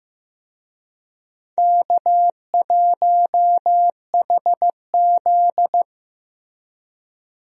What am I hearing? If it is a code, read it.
Morse code, heard as a keyed tone, about 16 wpm, 700 Hz. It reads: K1HZ
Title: USA Call Signs - 4 Characters - Repeated 2x - 15wpm